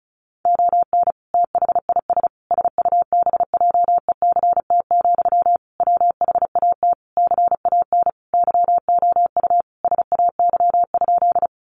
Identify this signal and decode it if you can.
Morse code, heard as a keyed tone, about 34 wpm, 700 Hz. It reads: ON THIS SUBJECT, WHAT CAN YOU SAY?
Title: Sentences from Top 800 Words 35wpm